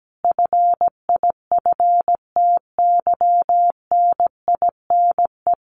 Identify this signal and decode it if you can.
Morse code, heard as a keyed tone, about 17 wpm, 700 Hz. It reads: FIFTYNINE
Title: A Sound Of Thunder 17wpm